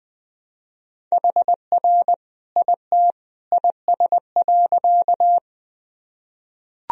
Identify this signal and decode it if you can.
Morse code, heard as a keyed tone, about 20 wpm, 700 Hz. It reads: HR IT IS.
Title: Sentences from Top 200 Words and Abbreviations 20wpm